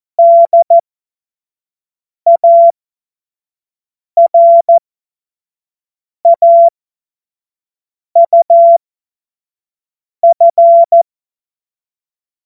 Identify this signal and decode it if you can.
Morse code, heard as a keyed tone, about 14 wpm, spaced as for 5 wpm, 680 Hz. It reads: DARAUF
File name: Morsekurs / Witze / Witz 11. Januar 2022, 25 F70